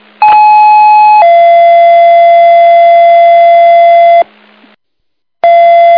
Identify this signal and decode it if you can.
Two-tone page, heard as a one-second tone, then a three-second tone, about 850 then 680 Hz, with noise. It is heard from 0.2 to 4.2 s.